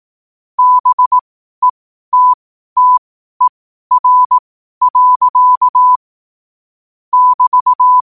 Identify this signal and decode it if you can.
Morse code, heard as a keyed tone, about 18 wpm, spaced as for 12 wpm, 1000 Hz. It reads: BETTER. <BT>
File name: qotdfiles / 12wpm